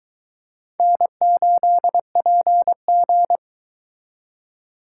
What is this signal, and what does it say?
Morse code, heard as a keyed tone, about 22 wpm, 700 Hz. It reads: N8PG